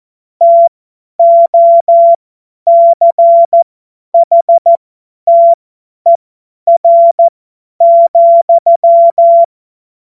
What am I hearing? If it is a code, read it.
Morse code, heard as a keyed tone, about 14 wpm, spaced as for 10 wpm, 680 Hz. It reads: TOCHTER,
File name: Morsekurs / Witze / Witz 05 März 2024 50 F70